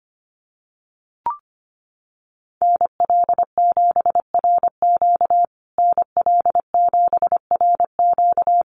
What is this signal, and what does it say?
Morse code, heard as a keyed tone, about 25 wpm, 700 Hz. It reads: NL7RQ NL7RQ
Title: USA Call Signs Repeated 3x 25wpm